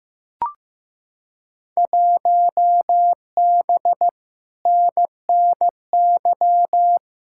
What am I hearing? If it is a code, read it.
Morse code, heard as a keyed tone, about 15 wpm, 700 Hz. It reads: 1B NNY
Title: Field Day Exchanges - 15wpm